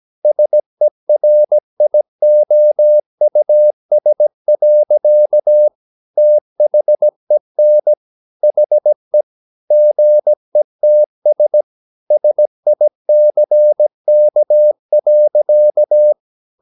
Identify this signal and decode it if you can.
Morse code, heard as a keyed tone, about 17 wpm, 600 Hz. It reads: SERIOUS. THEN HE GETS SICK.
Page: Romanen Never af Ken Follet på engelsk.